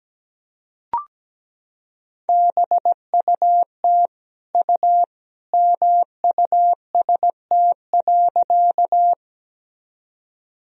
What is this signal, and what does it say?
Morse code, heard as a keyed tone, about 17 wpm, 700 Hz. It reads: BUT U MUST.
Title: Sentences from Top 200 Words and Abbreviations 17wpm